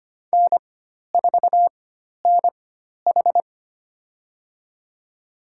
Morse code, heard as a keyed tone, about 25 wpm, spaced as for 12 wpm, 700 Hz. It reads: N4NH
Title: USA Call Signs - 4 Characters - 4x Character Spacing - 25wpm